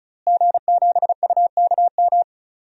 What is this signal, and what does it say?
Morse code, heard as a keyed tone, about 36 wpm, 700 Hz. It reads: G7UKM
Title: DXCC-100 Call Signs - 5 Characters - Repeated 2x - 35wpm